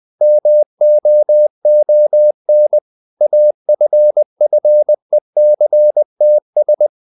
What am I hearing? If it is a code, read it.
Morse code, heard as a keyed tone, about 20 wpm, 600 Hz. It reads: MOON AFFECTS
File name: the old man and the sea / the old man and the sea part 1 20wpm